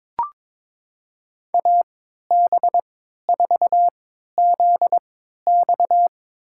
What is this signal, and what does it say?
Morse code, heard as a keyed tone, about 22 wpm, spaced as for 12 wpm, 700 Hz. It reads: AB4ZX
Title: USA Call Signs - 5 Characters - 3x Character Spacing - 22wpm